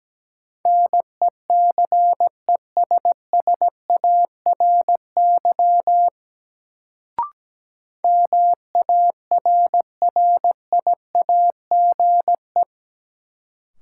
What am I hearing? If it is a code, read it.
Morse code, heard as a keyed tone, about 17 wpm, 700 Hz. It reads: NECESSARY MARRIAGE
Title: Sentences from Top 800 Words - New Word Drill - 17wpm